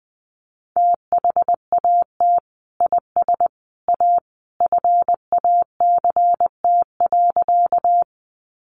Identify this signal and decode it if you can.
Morse code, heard as a keyed tone, about 19 wpm, 700 Hz. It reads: THAT IS A FACT.
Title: Sentences from Top 500 Words 20wpm